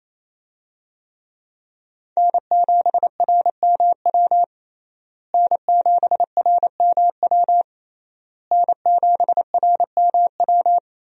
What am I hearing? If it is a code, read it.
Morse code, heard as a keyed tone, about 28 wpm, 700 Hz. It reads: N7RMW N7RMW N7RMW
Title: USA Call Signs - 5 Characters - Repeated 3x - 28wpm